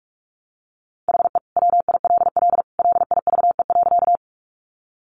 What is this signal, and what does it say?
Morse code, heard as a keyed tone, about 44 wpm, 700 Hz. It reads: HE WILL LIVE.
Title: Sentences from Top 200 Words 45wpm